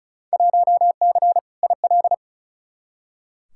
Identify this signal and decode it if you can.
Morse code, heard as a keyed tone, about 34 wpm, 700 Hz. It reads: 1C IL
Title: Field Day Exchanges - 35wpm